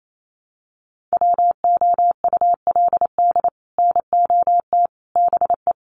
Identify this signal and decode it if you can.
Morse code, heard as a keyed tone, about 28 wpm, 700 Hz. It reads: WOULD NOT BE
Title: Sets of 5 Words 28wpm